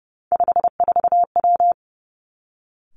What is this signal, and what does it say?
Morse code, heard as a keyed tone, about 30 wpm, 700 Hz. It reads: 54W
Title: Three Challenging Letter-Number Combinations 30wpm